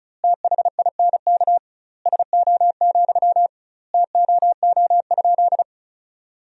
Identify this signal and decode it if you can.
Morse code, heard as a keyed tone, about 36 wpm, 700 Hz. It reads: THINK SO, TOO?
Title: 2x Word-Spacing - Sentences from Top 200 Words - 35wpm